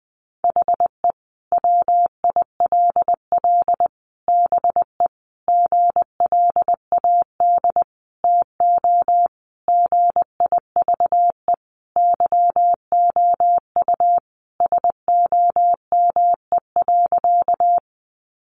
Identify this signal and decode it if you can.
Morse code, heard as a keyed tone, about 20 wpm, 700 Hz. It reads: HE WILL BE GLAD TO GIVE YOU SOME.
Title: Sentences from Top 800 Words 20wpm